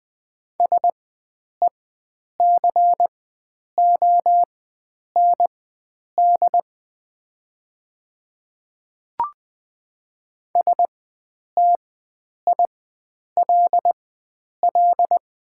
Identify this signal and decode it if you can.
Morse code, heard as a keyed tone, about 20 wpm, spaced as for 9 wpm, 700 Hz. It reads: SECOND STILL
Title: 4x Character Spacing - Top 300 Words - 20wpm